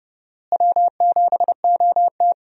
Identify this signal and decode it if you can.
Morse code, heard as a keyed tone, about 31 wpm, 700 Hz. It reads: W7OT
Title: USA Call Signs - 4 Characters - Repeated 2x - 30wpm